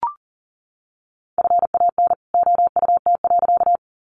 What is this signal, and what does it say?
Morse code, heard as a keyed tone, about 40 wpm, 700 Hz. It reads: FAN OUT.
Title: Sentences from Top 1300 Words 40wpm